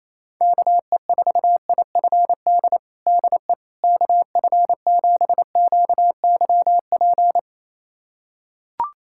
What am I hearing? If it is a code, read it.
Morse code, heard as a keyed tone, about 28 wpm, 700 Hz. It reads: KE4IFD DE KF7QYP
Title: USA Callsign DE Callsign - 6 Characters - 28wpm